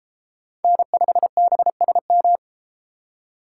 Morse code, heard as a keyed tone, about 30 wpm, 700 Hz. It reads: N5BSM